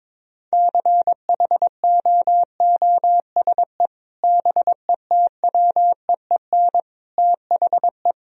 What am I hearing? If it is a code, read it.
Morse code, heard as a keyed tone, about 22 wpm, 700 Hz. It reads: CHOOSE BETWEEN THE
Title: Sentences from Top 1000 Words - Review Part 4 - 22wpm